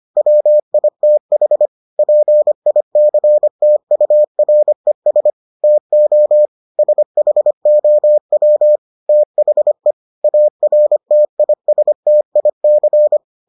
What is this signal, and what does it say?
Morse code, heard as a keyed tone, about 25 wpm, 600 Hz. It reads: WITH PICTURES TO SHOW THE ARTISTIC